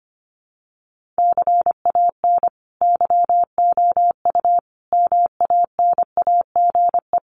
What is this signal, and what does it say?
Morse code, heard as a keyed tone, about 25 wpm, 700 Hz. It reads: CAN YOU MANAGE
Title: Sentences from Top 1400 Words 25wpm